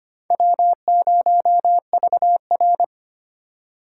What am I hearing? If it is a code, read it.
Morse code, heard as a keyed tone, about 32 wpm, 700 Hz. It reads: W0VR